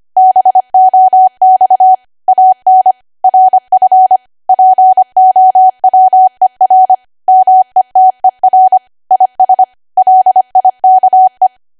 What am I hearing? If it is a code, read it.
Morse code, heard as a keyed tone, about 25 wpm, 750 Hz. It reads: BOX AN RF POWER METER IS LIKE
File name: cwa / W1AW-Level3 / 25WPM